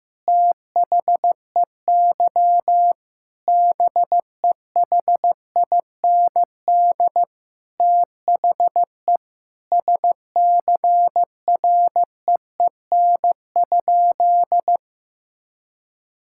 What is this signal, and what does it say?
Morse code, heard as a keyed tone, about 15 wpm, 700 Hz. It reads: THEY BEHIND THE SCREEN?
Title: Sentences from Top 1000 Words - Review Part 6 - 15wpm